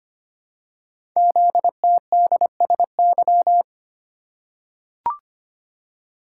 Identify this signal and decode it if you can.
Morse code, heard as a keyed tone, about 25 wpm, 700 Hz. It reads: ZTDSY
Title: Code Groups - 5 Letters - 25wpm